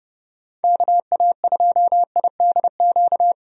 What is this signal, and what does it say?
Morse code, heard as a keyed tone, about 30 wpm, 700 Hz. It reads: KA2IDQ
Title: USA Call Signs - 6 Characters - 30wpm